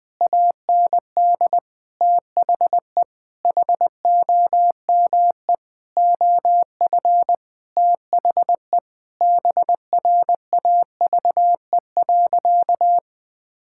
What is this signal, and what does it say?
Morse code, heard as a keyed tone, about 21 wpm, 700 Hz. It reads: AND THE HOME OF THE BRAVE.
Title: Sentences with 9-Word Phrases - 20wpm